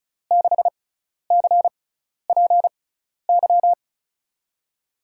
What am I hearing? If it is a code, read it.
Morse code, heard as a keyed tone, about 34 wpm, spaced as for 12 wpm, 700 Hz. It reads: 6CPY
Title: USA Call Signs - 6x Character Spacing - 35wpm